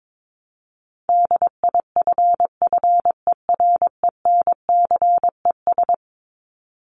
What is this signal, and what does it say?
Morse code, heard as a keyed tone, about 22 wpm, 700 Hz. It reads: DIFFERENCES
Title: ES:S Suffix - Top 100 Words with Base Words 22wpm